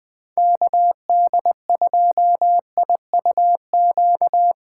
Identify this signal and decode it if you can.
Morse code, heard as a keyed tone, about 22 wpm, 700 Hz. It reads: KD2IUQ
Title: Speed Racing - USA 2x3 Call Signs 20wpm